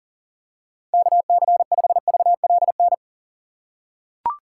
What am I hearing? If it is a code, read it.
Morse code, heard as a keyed tone, about 37 wpm, 700 Hz. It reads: KC5VLN
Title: USA Call Signs - 6 Characters - 3x Character Spacing - 40wpm